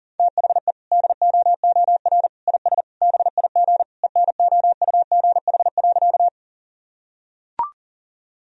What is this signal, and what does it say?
Morse code, heard as a keyed tone, about 40 wpm, 700 Hz. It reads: THE DOOR IS BIG ENOUGH.
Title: Sentences from Top 1000 Words - Review Part 1 - 40wpm